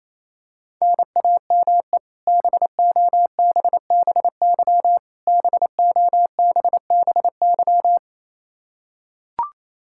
Morse code, heard as a keyed tone, about 28 wpm, 700 Hz. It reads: NAME BOBBY BOBBY
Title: Practice QSO - Exchange 1 - Beginning - 28wpm